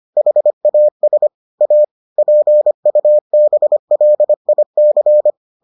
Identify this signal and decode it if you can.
Morse code, heard as a keyed tone, about 25 wpm, 600 Hz. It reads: HAS A PUBLIC